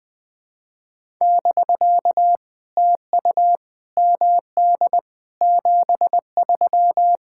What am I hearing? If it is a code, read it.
Morse code, heard as a keyed tone, about 20 wpm, 700 Hz. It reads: <BK> TU MD 73
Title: POTA QSO Practice - 20wpm